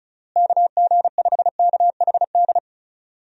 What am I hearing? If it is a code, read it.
Morse code, heard as a keyed tone, about 36 wpm, 700 Hz. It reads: KG5KHD
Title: USA Call Signs - 6 Characters - Repeated 3x - 35wpm